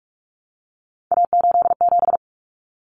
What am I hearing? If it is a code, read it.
Morse code, heard as a keyed tone, about 44 wpm, 700 Hz. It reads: A87
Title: Common Cut Numbers - 3-Digit 45wpm